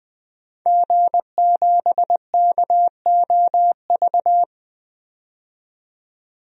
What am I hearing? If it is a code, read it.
Morse code, heard as a keyed tone, about 20 wpm, 700 Hz. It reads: G7KOV
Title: DXCC-100 Call Signs - 5 Characters - 20wpm